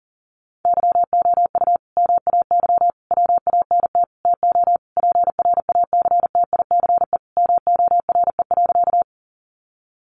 Morse code, heard as a keyed tone, about 40 wpm, 700 Hz. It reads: YOU MAY WANT TO PRACTICE MORE.
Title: Sentences from Top 700 Words 40wpm